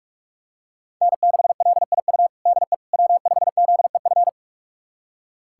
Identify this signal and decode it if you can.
Morse code, heard as a keyed tone, about 46 wpm, 700 Hz. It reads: N6LIU DE W5ZEF
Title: USA Callsign DE Callsign - 5 Characters - 45wpm